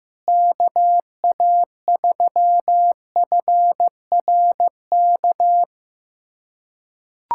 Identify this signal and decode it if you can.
Morse code, heard as a keyed tone, about 15 wpm, 700 Hz. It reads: KA3FRK
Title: USA Callsign DE Callsign - 6 Characters - 15wpm